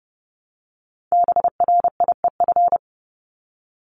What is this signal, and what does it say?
Morse code, heard as a keyed tone, about 30 wpm, 700 Hz. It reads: BRIEF
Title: LY Suffix - Top 100 Words with Base Words 30wpm